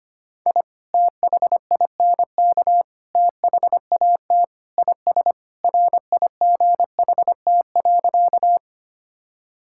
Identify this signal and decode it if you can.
Morse code, heard as a keyed tone, about 25 wpm, 700 Hz. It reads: I THINK THAT IS RIGHT.
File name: practice / 25wpm / Sentences with 3-Word Phrases - 25wpm